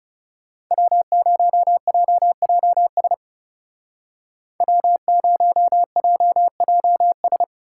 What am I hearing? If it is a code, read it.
Morse code, heard as a keyed tone, about 36 wpm, 700 Hz. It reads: W0JJS W0JJS
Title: Speed Racing - USA 1x3 Call Signs 30wpm